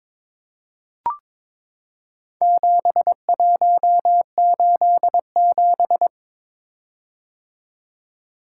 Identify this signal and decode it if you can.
Morse code, heard as a keyed tone, about 22 wpm, 700 Hz. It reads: 7187
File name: practice / 22wpm / Four Letter-Number Combinations - 22wpm